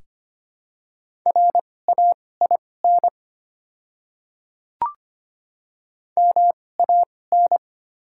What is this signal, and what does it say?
Morse code, heard as a keyed tone, about 25 wpm, spaced as for 18 wpm, 700 Hz. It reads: RAIN MAN
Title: 2x Character Spacing - Top 300 Words - 25wpm